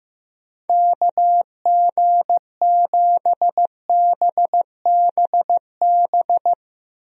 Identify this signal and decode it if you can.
Morse code, heard as a keyed tone, about 15 wpm, 700 Hz. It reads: KG7BBB
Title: USA Call Signs - 6 Characters - Repeated 2x - 15wpm